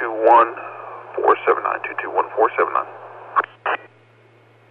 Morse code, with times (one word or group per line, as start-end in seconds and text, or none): none